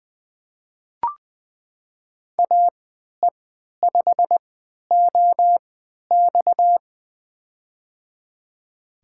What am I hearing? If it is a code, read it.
Morse code, heard as a keyed tone, about 20 wpm, spaced as for 11 wpm, 700 Hz. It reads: AE5OX
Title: USA Call Signs - 3x Character Spacing - 20wpm